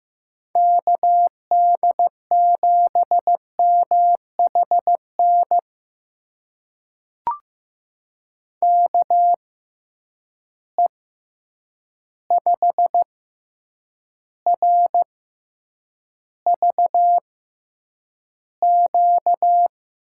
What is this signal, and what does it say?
Morse code, heard as a keyed tone, about 15 wpm, spaced as for 8 wpm, 700 Hz. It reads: KD7MHN KE5RVQ
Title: USA Call Signs - 6 Characters - 6x Character Spacing - 15wpm